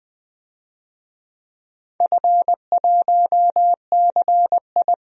Morse code, heard as a keyed tone, about 20 wpm, 700 Hz. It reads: F1CI
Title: DXCC-100 Call Signs - 4 Characters - Repeated 3x - 20wpm